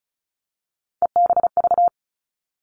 Morse code, heard as a keyed tone, about 36 wpm, 700 Hz. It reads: EBV
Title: Three Challenging Letter-Number Combinations 35wpm